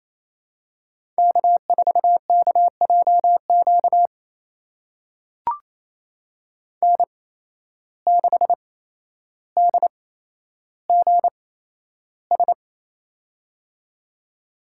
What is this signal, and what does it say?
Morse code, heard as a keyed tone, about 28 wpm, spaced as for 12 wpm, 700 Hz. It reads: K4KJQ N6DGS
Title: USA Call Signs - 5 Characters - 8x Character Spacing - 28wpm